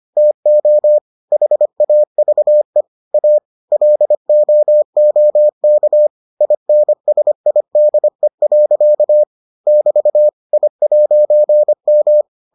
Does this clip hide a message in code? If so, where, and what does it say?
Yes: Morse code from 0.2 to 12.2 s, heard as a keyed tone, about 25 wpm, 600 Hz. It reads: TO HAVE A LOOK INSIDE. <BT> I'M